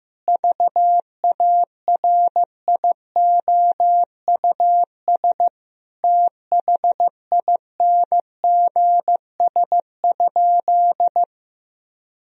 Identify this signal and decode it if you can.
Morse code, heard as a keyed tone, about 15 wpm, 700 Hz. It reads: VARIOUS THINGS?
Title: Sentences from Top 1000 Words - Review Part 4 - 15wpm